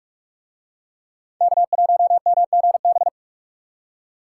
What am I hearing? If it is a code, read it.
Morse code, heard as a keyed tone, about 46 wpm, 700 Hz. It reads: K1MGB